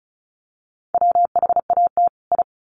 Morse code, heard as a keyed tone, about 34 wpm, 700 Hz. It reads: WHAT I
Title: Sets of 2 Words 35wpm